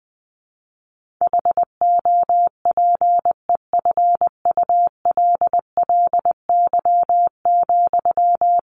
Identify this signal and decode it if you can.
Morse code, heard as a keyed tone, about 20 wpm, 700 Hz. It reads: HOPEFULLY,
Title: Sentences from Top 1500 Words 20wpm